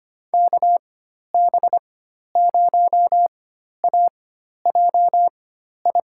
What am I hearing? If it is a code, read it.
Morse code, heard as a keyed tone, about 25 wpm, spaced as for 12 wpm, 700 Hz. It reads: KB0AJI
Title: USA Call Signs - 4x Character Spacing - 25wpm